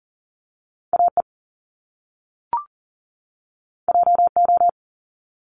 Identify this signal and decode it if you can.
Morse code, heard as a keyed tone, about 39 wpm, 700 Hz. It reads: AE JO
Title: Top Two Letter Combinations in Words 40wpm